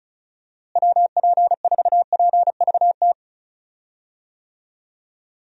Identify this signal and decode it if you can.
Morse code, heard as a keyed tone, about 36 wpm, 700 Hz. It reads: WP4PVT
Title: USA Call Signs - 6 Characters - Repeated 2x - 35wpm